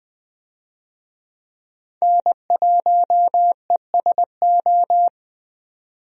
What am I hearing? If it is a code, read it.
Morse code, heard as a keyed tone, about 20 wpm, 700 Hz. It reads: N1ESO
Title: USA Call Signs - 5 Characters - 20wpm